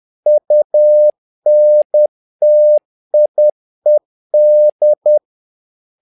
Morse code, heard as a keyed tone, about 10 wpm, 600 Hz. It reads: UNTIED
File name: learning center / tower trouble / tower trouble 10wpm